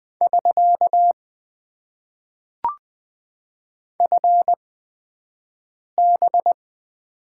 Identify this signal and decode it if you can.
Morse code, heard as a keyed tone, about 21 wpm, spaced as for 5 wpm, 700 Hz. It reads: <SK> FB
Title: Top 90 QSO Elements - 8x Character Spacing - 20wpm